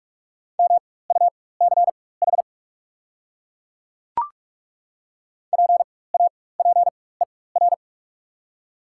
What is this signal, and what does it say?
Morse code, heard as a keyed tone, about 48 wpm, spaced as for 21 wpm, 690 Hz. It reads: MUCH PAPER
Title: 4x Character Spacing - Top 300 Words - 45wpm